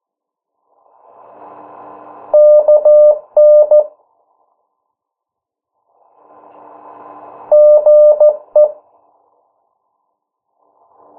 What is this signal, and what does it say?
Morse code, heard as a keyed tone, about 14 wpm, 600 Hz. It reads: KN GE
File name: morse mp3 / 09 urok 8